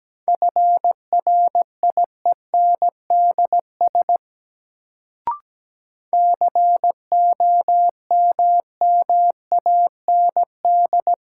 Morse code, heard as a keyed tone, about 17 wpm, 700 Hz. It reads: FRIENDS COMMAND
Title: Sentences from Top 1600 Words - New Word Drill Alt - 17wpm